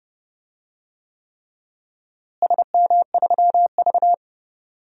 Morse code, heard as a keyed tone, about 30 wpm, 700 Hz. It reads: SM3V
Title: DXCC-100 Call Signs - 4 Characters - Repeated 3x - 30wpm